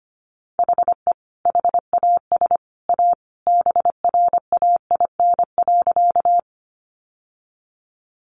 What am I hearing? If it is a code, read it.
Morse code, heard as a keyed tone, about 25 wpm, 700 Hz. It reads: HE HAS A BRAIN.
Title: Sentences from Top 1100 Words 25wpm